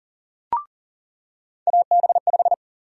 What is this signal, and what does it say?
Morse code, heard as a keyed tone, about 40 wpm, 700 Hz. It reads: AB5
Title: USA Call Sign Prefixes - 40wpm